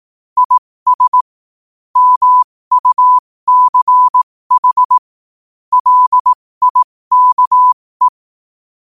Morse code, heard as a keyed tone, about 18 wpm, spaced as for 15 wpm, 1000 Hz. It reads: IS MUCH LIKE
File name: qotdfiles / 15wpm